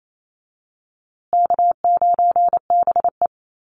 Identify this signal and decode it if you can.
Morse code, heard as a keyed tone, about 28 wpm, 700 Hz. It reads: K9BE
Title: USA Call Signs Repeated 2x 28wpm